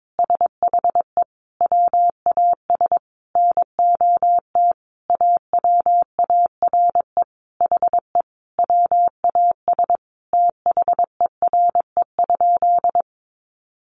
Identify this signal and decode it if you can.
Morse code, heard as a keyed tone, about 22 wpm, 700 Hz. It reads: SHE WAS NOT AWARE HE WAS THERE?
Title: Sentences from Top 1200 Words 22wpm